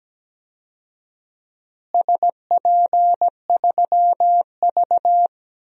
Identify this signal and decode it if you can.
Morse code, heard as a keyed tone, about 17 wpm, 700 Hz. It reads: SP3V
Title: DXCC-100 Call Signs - 4 Characters - Repeated 3x - 17wpm